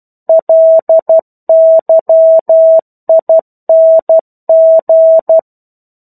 Morse code, heard as a keyed tone, about 12 wpm, 650 Hz. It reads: LYING
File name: 01-Chapter-01 12 WPM